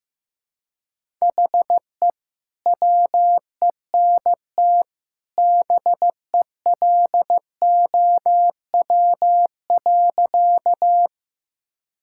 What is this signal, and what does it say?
Morse code, heard as a keyed tone, about 15 wpm, 700 Hz. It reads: HE WENT BELOW.
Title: Sentences from Top 1000 Words - Review Part 5 - 15wpm